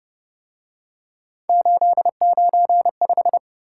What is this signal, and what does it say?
Morse code, heard as a keyed tone, about 30 wpm, 700 Hz. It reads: 895
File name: Speed-Racing - Three Letter-Number Combinations - 25wpm